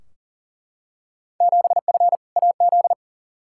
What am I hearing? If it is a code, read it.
Morse code, heard as a keyed tone, about 40 wpm, 700 Hz. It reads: 7F AZ
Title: Field Day Exchanges Encore - 40wpm